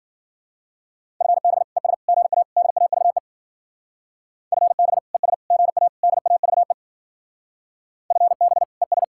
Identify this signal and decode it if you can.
Morse code, heard as a keyed tone, about 50 wpm, 700 Hz. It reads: FB ES GA DAVE FB ES GA DAVE FB ES
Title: Speed-Racing Common QSO Phrases - 40wpm